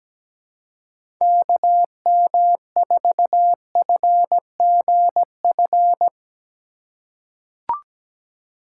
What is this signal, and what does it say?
Morse code, heard as a keyed tone, about 17 wpm, 700 Hz. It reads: KM4FGF
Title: USA Call Signs - 6 Characters - 4x Character Spacing - 17wpm